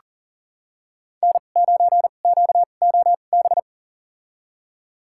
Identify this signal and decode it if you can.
Morse code, heard as a keyed tone, about 40 wpm, spaced as for 29 wpm, 700 Hz. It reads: N9QOB
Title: USA Call Signs - 2x Character Spacing - 40wpm